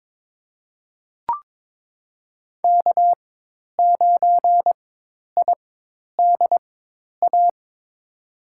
Morse code, heard as a keyed tone, about 22 wpm, spaced as for 10 wpm, 700 Hz. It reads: K9IDA
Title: USA Call Signs - 4x Character Spacing - 22wpm